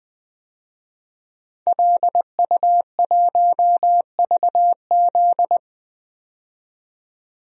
Morse code, heard as a keyed tone, about 20 wpm, 700 Hz. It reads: LU1VZ
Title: DXCC-100 Call Signs - 5 Characters - 20wpm